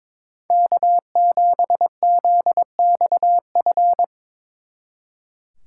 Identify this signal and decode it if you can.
Morse code, heard as a keyed tone, about 24 wpm, 700 Hz. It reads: K7ZXF
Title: Speed Racing - USA Call Signs 22wpm